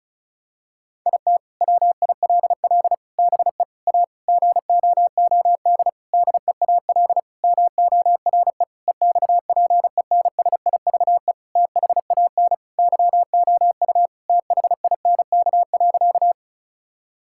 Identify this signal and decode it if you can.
Morse code, heard as a keyed tone, about 36 wpm, 700 Hz. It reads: IT WILL BE A GOOD DEAL MORE EXPENSIVE THAN YOU THINK.
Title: Sentences from Top 1000 Words - Review Part 6 - 35wpm